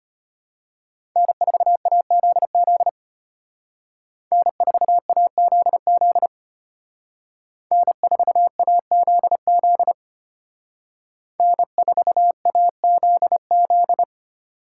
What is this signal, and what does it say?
Morse code, heard as a keyed tone, about 32 wpm, 700 Hz. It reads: N4AZZ N4AZZ N4AZZ N4AZZ